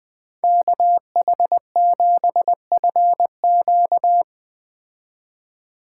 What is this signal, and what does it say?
Morse code, heard as a keyed tone, about 20 wpm, 700 Hz. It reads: KH7FQ
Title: Speed Racing - USA 2x2 Call Signs 17wpm